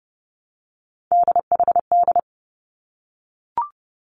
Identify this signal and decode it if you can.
Morse code, heard as a keyed tone, about 30 wpm, 700 Hz. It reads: DHD
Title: Three Challenging Letter-Number Combinations 30wpm